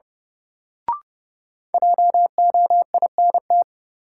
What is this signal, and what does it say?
Morse code, heard as a keyed tone, about 30 wpm, 700 Hz. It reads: JOINT
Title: Sentences from Top 1300 Words - New Word Drill - 30wpm